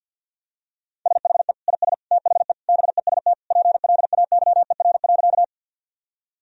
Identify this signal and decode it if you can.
Morse code, heard as a keyed tone, about 50 wpm, 700 Hz. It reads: SHE IS THE BEST PLAYER.
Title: Sentences from Top 1000 Words - Review Part 4 - 50wpm